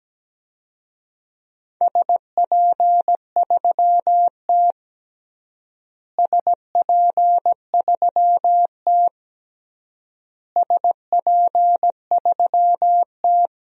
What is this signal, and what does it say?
Morse code, heard as a keyed tone, about 17 wpm, 700 Hz. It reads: SP3T SP3T SP3T
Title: DXCC-100 Call Signs - 4 Characters - Repeated 3x - 17wpm